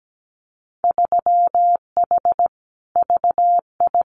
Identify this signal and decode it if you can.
Morse code, heard as a keyed tone, about 17 wpm, 700 Hz. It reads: 3H VI
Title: Winter Field Day Exchanges Encore 17wpm